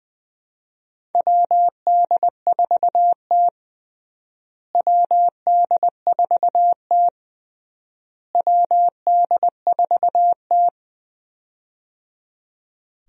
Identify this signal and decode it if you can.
Morse code, heard as a keyed tone, about 20 wpm, 700 Hz. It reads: WD4T WD4T WD4T
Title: USA Call Signs - 4 Characters - Repeated 3x - 20wpm